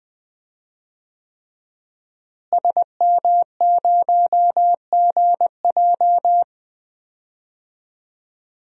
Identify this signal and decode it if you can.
Morse code, heard as a keyed tone, about 20 wpm, 700 Hz. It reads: SM0GJ
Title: DXCC-100 Call Signs - 5 Characters - Repeated 3x - 20wpm